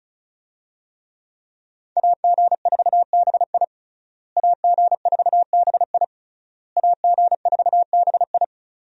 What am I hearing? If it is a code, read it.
Morse code, heard as a keyed tone, about 36 wpm, 700 Hz. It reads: AG4BI AG4BI AG4BI
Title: USA Call Signs - 5 Characters - Repeated 3x - 35wpm